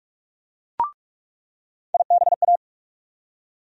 Morse code, heard as a keyed tone, about 41 wpm, 700 Hz. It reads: IBA
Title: USA Call Sign Postfixes - 3 Letters 45wpm